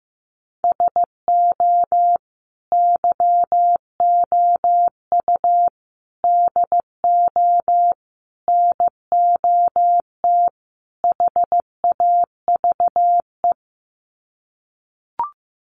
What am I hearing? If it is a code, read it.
Morse code, heard as a keyed tone, about 15 wpm, 700 Hz. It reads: SO YOU DO NOT HAVE
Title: Sets of 5 Words Encore 15wpm